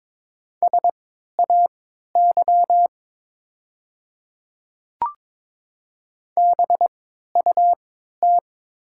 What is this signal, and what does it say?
Morse code, heard as a keyed tone, about 22 wpm, spaced as for 13 wpm, 700 Hz. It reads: SAY BUT
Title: 3x Character Spacing - Top 200 Words - 22wpm